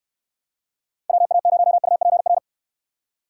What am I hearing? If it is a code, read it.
Morse code, heard as a keyed tone, about 63 wpm, 700 Hz. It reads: WA0UPR